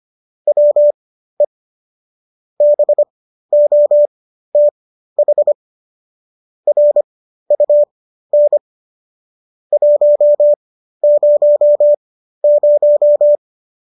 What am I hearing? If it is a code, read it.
Morse code, heard as a keyed tone, about 25 wpm, spaced as for 13 wpm, 590 Hz. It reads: WE BOTH RUN 100